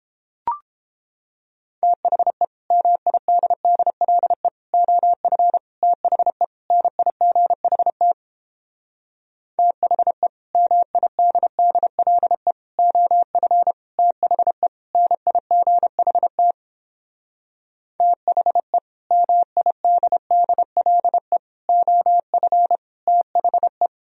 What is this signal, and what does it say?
Morse code, heard as a keyed tone, about 29 wpm, 700 Hz. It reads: THE MIDDLE OF THE NIGHT THE MIDDLE OF THE NIGHT THE MIDDLE OF THE